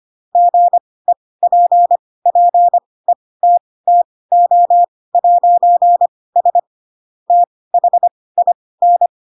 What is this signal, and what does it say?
Morse code, heard as a keyed tone, about 25 wpm, spaced as for 18 wpm, 700 Hz. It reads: GEPPETTO'S THIN
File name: Pinocchio / Pinocchio ch2 12 25 ik1hge